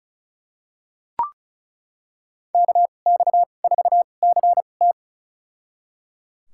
Morse code, heard as a keyed tone, about 36 wpm, spaced as for 25 wpm, 700 Hz. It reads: KX4CT
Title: USA Call Signs - 5 Characters - 2x Character Spacing - 35wpm